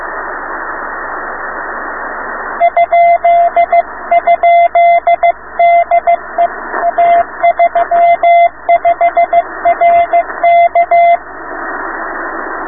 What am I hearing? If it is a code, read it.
Morse code, heard as a keyed tone, about 15 wpm, 690 Hz, with radio noise. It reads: ?? DE A35RK